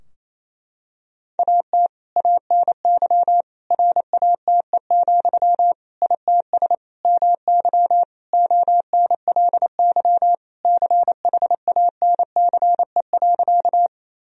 Morse code, heard as a keyed tone, about 28 wpm, 700 Hz. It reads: AT ANY RATE, ITS MY ONLY CHANCE.
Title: Sentences from Top 1000 Words - Review Part 4 - 28wpm